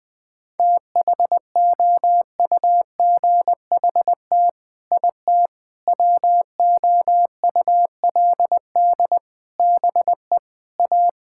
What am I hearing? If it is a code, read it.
Morse code, heard as a keyed tone, about 20 wpm, 700 Hz. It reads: THOUGHT IT WOULD BE A
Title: Sets of 8 Words - 20wpm